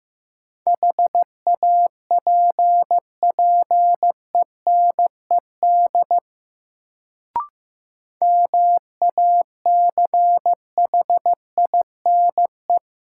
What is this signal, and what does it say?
Morse code, heard as a keyed tone, about 16 wpm, 700 Hz. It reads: HAPPENED MACHINE
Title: Sentences from Top 1000 Words - Review Part 2 - New Word Drill - 15wpm